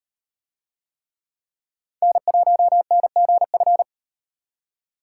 Morse code, heard as a keyed tone, about 38 wpm, 700 Hz. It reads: N1NGF